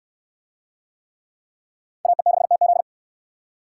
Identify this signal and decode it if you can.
Morse code, heard as a keyed tone, about 69 wpm, 700 Hz. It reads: AE6TZ